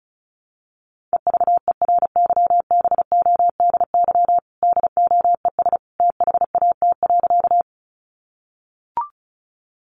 Morse code, heard as a keyed tone, about 34 wpm, 700 Hz. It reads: EVERYBODY DOES THAT.
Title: Sentences from Top 1400 Words 35wpm